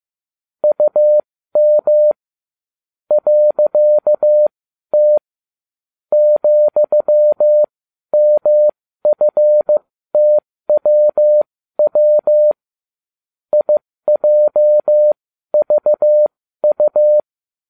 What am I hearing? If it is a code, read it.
Morse code, heard as a keyed tone, about 15 wpm, spaced as for 12 wpm, 600 Hz. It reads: UM .T ,MFTWW IJVU